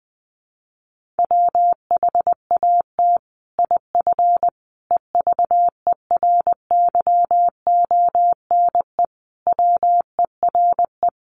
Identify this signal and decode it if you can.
Morse code, heard as a keyed tone, about 20 wpm, 700 Hz. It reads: WHAT IF EVERYONE WERE
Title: Sentences from Top 1400 Words 20wpm